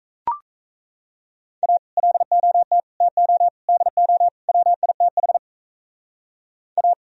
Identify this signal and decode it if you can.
Morse code, heard as a keyed tone, about 38 wpm, 700 Hz. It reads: A LOT TO DO WITH A